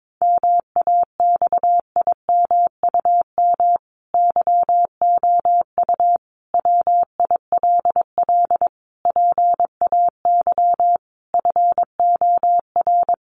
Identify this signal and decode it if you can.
Morse code, heard as a keyed tone, about 22 wpm, 700 Hz. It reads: MAXIMUM YOU WILL PAY FOR
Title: Sentences from Top 1300 Words 22wpm